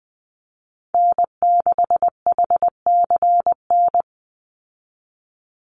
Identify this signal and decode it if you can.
Morse code, heard as a keyed tone, about 20 wpm, 700 Hz. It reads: N6HCN
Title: Call Signs Repeated 2x - 20wpm